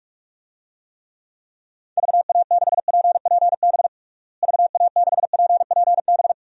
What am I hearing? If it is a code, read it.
Morse code, heard as a keyed tone, about 41 wpm, 700 Hz. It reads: VA6PPB VA6PPB
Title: DXCC-100 Call Signs - 6 Characters - Repeated 2x - 45wpm